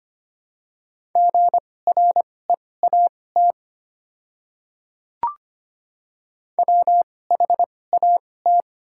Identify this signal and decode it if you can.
Morse code, heard as a keyed tone, about 25 wpm, spaced as for 18 wpm, 690 Hz. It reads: GREAT WHAT
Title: 2x Character Spacing - Top 200 Words - 25wpm